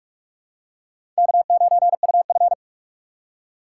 Morse code, heard as a keyed tone, about 45 wpm, 700 Hz. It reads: K9UF
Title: USA Call Signs - 4 Characters - 2x Character Spacing - 45wpm